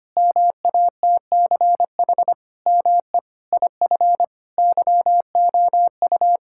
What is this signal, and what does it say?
Morse code, heard as a keyed tone, about 25 wpm, 700 Hz. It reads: MATCH ME IF YOU
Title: Sentences from Top 1600 Words 25wpm